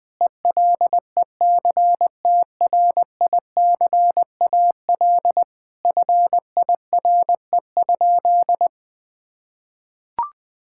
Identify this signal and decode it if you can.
Morse code, heard as a keyed tone, about 20 wpm, 700 Hz. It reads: ELECTRICAL FIRE?
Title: Sentences from Top 1000 Words - Review Part 6 - 20wpm